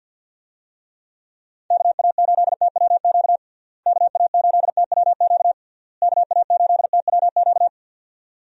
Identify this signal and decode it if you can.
Morse code, heard as a keyed tone, about 52 wpm, 700 Hz. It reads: KA8TWQ KA8TWQ KA8TWQ
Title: USA Call Signs - 6 Characters - Repeated 3x - 50wpm